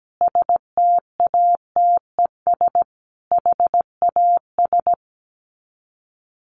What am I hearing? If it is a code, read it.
Morse code, heard as a keyed tone, about 17 wpm, 700 Hz. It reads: STATES HAS
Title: Sets of 4 Words 17wpm